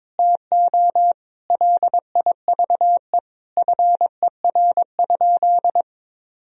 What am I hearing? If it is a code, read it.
Morse code, heard as a keyed tone, about 22 wpm, 700 Hz. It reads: TO LIVE FER?
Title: Sentences from Top 200 Words and Abbreviations 22wpm